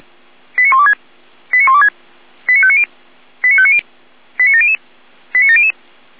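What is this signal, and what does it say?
Five-tone selective call (ZVEI), heard as quick runs of five tones, tones of about 70 ms. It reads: 89126 89126 89590 89590 89700 89700
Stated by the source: ZVEI five-tone selective call received by SDR.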